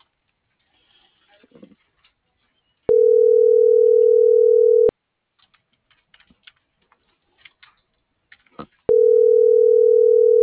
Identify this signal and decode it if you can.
Ringback tone (North America), 440 + 480 Hz.